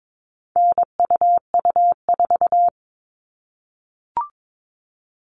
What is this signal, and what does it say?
Morse code, heard as a keyed tone, about 22 wpm, 700 Hz. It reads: NUU4
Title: All Cut Numbers - 4-Digit 22wpm